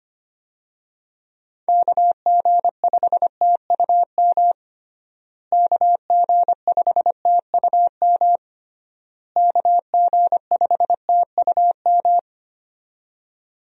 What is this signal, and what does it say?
Morse code, heard as a keyed tone, about 25 wpm, 700 Hz. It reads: KG5TUM KG5TUM KG5TUM
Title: USA Call Signs - 6 Characters - Repeated 3x - 25wpm